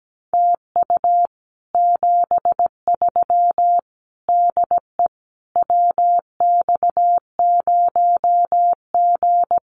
Morse code, heard as a keyed tone, about 17 wpm, 700 Hz. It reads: TU 73 DE WX0G
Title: Common QSO Phrases Encore 17wpm